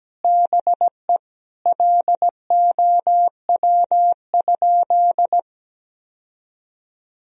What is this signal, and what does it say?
Morse code, heard as a keyed tone, about 17 wpm, 700 Hz. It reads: BE LOW?
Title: Sentences from Top 1000 Words - Review Part 1 - 17wpm